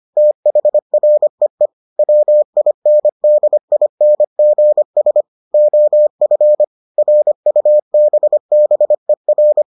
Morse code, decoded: THREE WINDINGS OF RUBBER

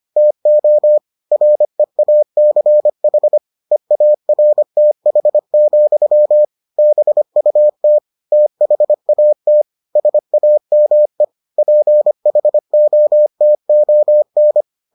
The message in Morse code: TO REACH EARTH, BUT THAT SAME PHOTON

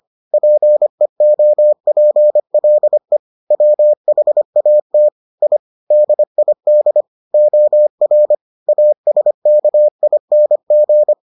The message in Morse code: PEOPLE WHAT I DID OR ASKING

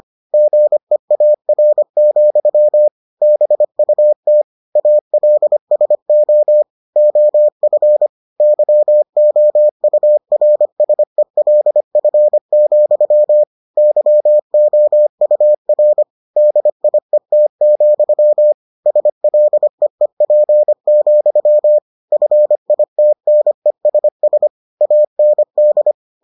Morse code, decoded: GEAR, BUT ALSO OF YOURSELF, YOUR DIET, SLEEP, FITNESS AND

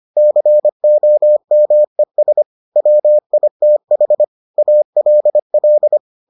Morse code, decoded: COMES WITH ALL

